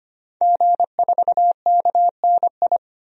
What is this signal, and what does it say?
Morse code, heard as a keyed tone, about 25 wpm, 700 Hz. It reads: G4KNI
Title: DXCC-100 Call Signs - 5 Characters - 25wpm